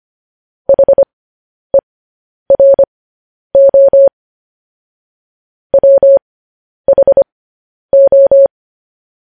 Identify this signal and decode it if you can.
Morse code, heard as a keyed tone, about 25 wpm, spaced as for 10 wpm, 550 Hz. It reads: HERO WHO
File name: podcast / News headlines-1x-25x10